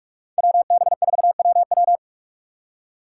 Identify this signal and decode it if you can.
Morse code, heard as a keyed tone, about 58 wpm, 700 Hz. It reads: WB4WW